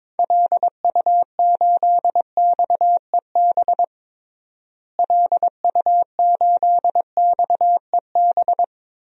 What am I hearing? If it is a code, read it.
Morse code, heard as a keyed tone, about 22 wpm, 700 Hz. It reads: LU8XEB LU8XEB
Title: DXCC-100 Call Signs - 6 Characters - Repeated 2x - 22wpm